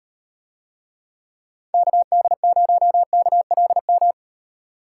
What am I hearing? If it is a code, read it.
Morse code, heard as a keyed tone, about 38 wpm, 700 Hz. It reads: KD0KLM